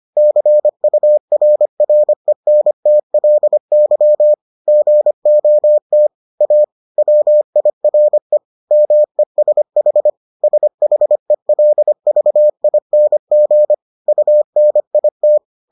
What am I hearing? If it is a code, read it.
Morse code, heard as a keyed tone, about 25 wpm, 600 Hz. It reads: CURRENTLY GOT A WIRE MESH SHELVING UNIT